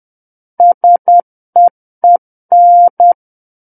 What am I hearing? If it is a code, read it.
Morse code, heard as a keyed tone, about 10 wpm, 700 Hz. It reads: SEEN
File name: Book One / 01-Chapter 1 - 10 WPM